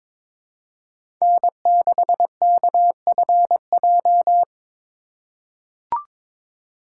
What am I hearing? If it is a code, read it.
Morse code, heard as a keyed tone, about 22 wpm, 700 Hz. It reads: N6KFJ
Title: USA Call Signs - 5 Characters - 2x Character Spacing - 22wpm